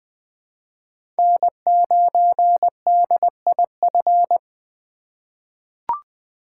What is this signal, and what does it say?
Morse code, heard as a keyed tone, about 20 wpm, 700 Hz. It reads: N9DIF
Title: USA Call Signs - 5 Characters - 3x Character Spacing - 20wpm